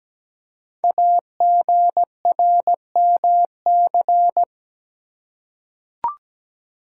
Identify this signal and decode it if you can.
Morse code, heard as a keyed tone, about 17 wpm, 700 Hz. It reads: AGRMC
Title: Code Groups - 5 Letters - 17wpm